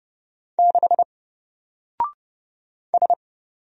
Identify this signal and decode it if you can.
Morse code, heard as a keyed tone, about 30 wpm, 700 Hz. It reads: 6 S
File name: Single-Letter-Number-30wpm